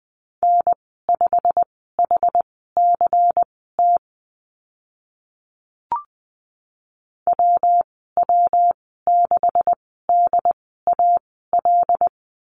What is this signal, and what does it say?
Morse code, heard as a keyed tone, about 20 wpm, spaced as for 15 wpm, 700 Hz. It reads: N5HCT WW6DAL
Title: 154 Lesson 25g - Letter p - Call Signs 20wpm